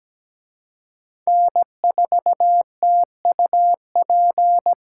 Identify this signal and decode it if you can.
Morse code, heard as a keyed tone, about 20 wpm, 700 Hz. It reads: N4TUP
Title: Speed Racing - USA 1x3 Call Signs 17wpm